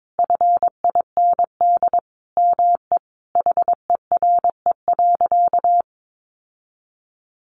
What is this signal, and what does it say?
Morse code, heard as a keyed tone, about 22 wpm, 700 Hz. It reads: FIND ME HERE.
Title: Sentences from Top 1200 Words 22wpm